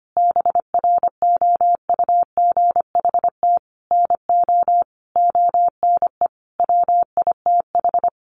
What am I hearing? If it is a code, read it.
Morse code, heard as a keyed tone, about 25 wpm, 700 Hz. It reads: BROUGHT NO ONE WITH
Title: Sentences from Top 500 Words 25wpm